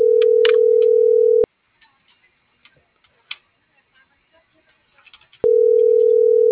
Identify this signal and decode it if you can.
Ringback tone (North America), 440 + 480 Hz.